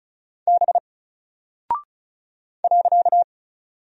Morse code, heard as a keyed tone, about 34 wpm, 700 Hz. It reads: B .
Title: Single Character (Letter:Number:Punctuation:Prosign) 35wpm